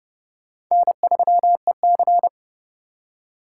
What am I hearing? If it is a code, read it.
Morse code, heard as a keyed tone, about 30 wpm, 700 Hz. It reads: N3EC